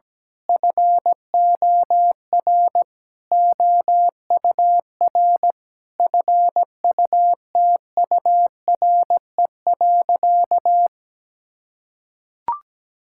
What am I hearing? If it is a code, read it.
Morse code, heard as a keyed tone, about 17 wpm, 700 Hz. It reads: FOR OUR FUTURE.
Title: Sentences from Top 1000 Words - Review Part 6 - 17wpm